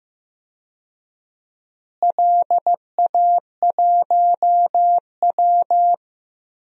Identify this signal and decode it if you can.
Morse code, heard as a keyed tone, about 15 wpm, 700 Hz. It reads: LA1W
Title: DXCC-100 Call Signs - 4 Characters - Repeated 3x - 15wpm